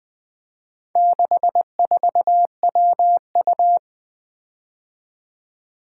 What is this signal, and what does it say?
Morse code, heard as a keyed tone, about 20 wpm, 700 Hz. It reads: 64WU
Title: Four Letter-Number Combinations - 20wpm